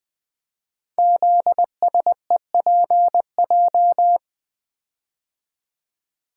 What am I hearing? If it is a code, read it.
Morse code, heard as a keyed tone, about 20 wpm, 700 Hz. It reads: ZSEPJ
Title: Code Groups - 5 Letters - 20wpm